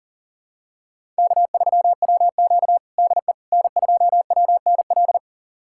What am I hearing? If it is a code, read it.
Morse code, heard as a keyed tone, about 40 wpm, 700 Hz. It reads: K3WQ DE N2WNL
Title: USA Callsign DE Callsign 40wpm